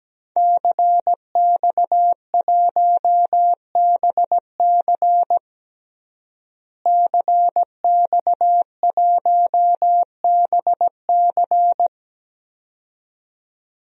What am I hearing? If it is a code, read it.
Morse code, heard as a keyed tone, about 17 wpm, 700 Hz. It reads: CX1BC CX1BC
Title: DXCC-100 Call Signs - 5 Characters - Repeated 2x - 17wpm